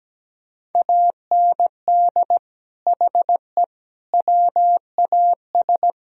Morse code, heard as a keyed tone, about 17 wpm, 700 Hz. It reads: AND HE WAS
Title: Speed-Racing - Sets of 3 Words 17wpm